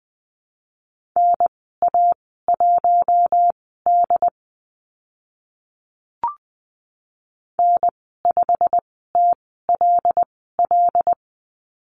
Morse code, heard as a keyed tone, about 20 wpm, spaced as for 15 wpm, 700 Hz. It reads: NA1D N5TLL
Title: 088 Lesson 15g - Number 5 - Call Signs 20wpm